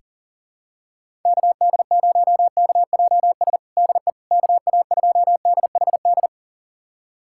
Morse code, heard as a keyed tone, about 40 wpm, 700 Hz. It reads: KD0KJS DE KA2DHD
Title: USA Callsign DE Callsign - 6 Characters - 40wpm